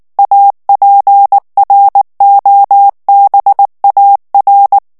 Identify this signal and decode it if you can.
Morse code, heard as a keyed tone, about 19 wpm, 800 Hz. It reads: APROBAR
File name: morse mp3 / ESPAGNOL TEXTE EN CLAIR 2 VITESSE 18 MOTS MINUTES